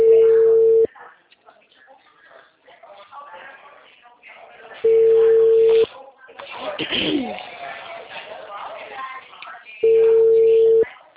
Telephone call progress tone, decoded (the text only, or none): ringback tone